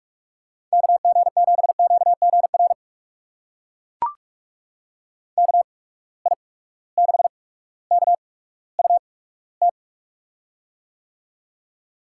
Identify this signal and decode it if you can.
Morse code, heard as a keyed tone, about 42 wpm, spaced as for 19 wpm, 700 Hz. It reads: KG7QGR KI6KUT
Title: USA Call Signs - 6 Characters - 8x Character Spacing - 45wpm